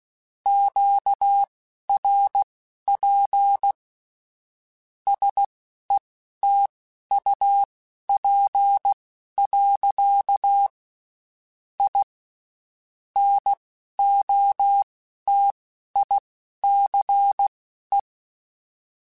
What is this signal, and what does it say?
Morse code, heard as a keyed tone, about 16 wpm, spaced as for 11 wpm, 800 Hz. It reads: QRP SETUP. I NOTICE
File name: (spaced) / QSOs Spaced-mp3s / WD0BGP 1 @15 sp